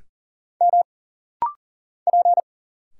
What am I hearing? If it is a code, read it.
Morse code, heard as a keyed tone, about 41 wpm, 700 Hz. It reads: M P
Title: Single Letters - Rapid-Fire 40wpm